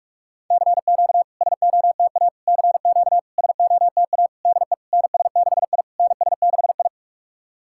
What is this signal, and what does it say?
Morse code, heard as a keyed tone, about 46 wpm, 690 Hz. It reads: CQ SOTA CQ SOTA DE NS6I NS6I
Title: SOTA QSO Practice 45wpm